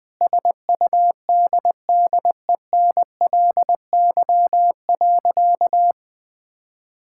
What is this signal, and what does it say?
Morse code, heard as a keyed tone, about 20 wpm, 700 Hz. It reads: SUDDENLY.
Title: Sentences with 2-Word Phrases - Spoken Beginning - 20wpm